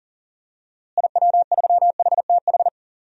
Morse code, heard as a keyed tone, about 40 wpm, 700 Hz. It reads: IW3HTH
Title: DXCC-100 Call Signs - 6 Characters - Repeated 3x - 40wpm